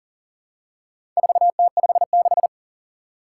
Morse code, heard as a keyed tone, about 39 wpm, 700 Hz. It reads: 4T56
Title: Four Challenging Letter-Number Combinations 40wpm